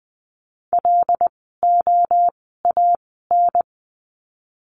Morse code, heard as a keyed tone, about 20 wpm, spaced as for 14 wpm, 700 Hz. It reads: LOAN
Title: 075 Lesson 13f - Letter L - Words 20wpm